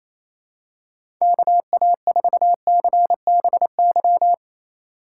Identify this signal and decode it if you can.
Morse code, heard as a keyed tone, about 28 wpm, 700 Hz. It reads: KA4CBY
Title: USA Call Signs - 6x Character Spacing - 28wpm